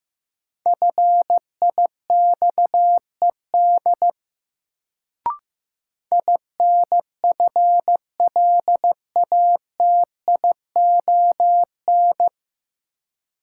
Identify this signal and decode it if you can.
Morse code, heard as a keyed tone, about 15 wpm, 700 Hz. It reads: FIXED INFLATION
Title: Sentences from Top 1100 Words - New Word Drill - 15wpm